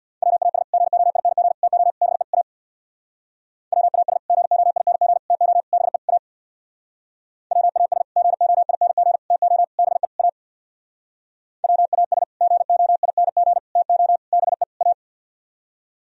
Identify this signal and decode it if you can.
Morse code, heard as a keyed tone, about 64 wpm, 700 Hz. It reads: WAS GOING TO BE A WAS GOING TO BE A WAS GOING TO BE A WAS GOING TO BE A